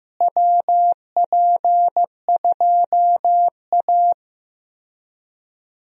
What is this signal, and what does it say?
Morse code, heard as a keyed tone, about 15 wpm, 700 Hz. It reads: WP2A
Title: USA Call Signs - 4 Characters - Repeated 3x - 15wpm